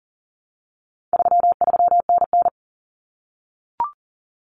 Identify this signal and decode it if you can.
Morse code, heard as a keyed tone, about 40 wpm, 700 Hz. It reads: 33NN
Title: Common Cut Numbers - 4-Digit 40wpm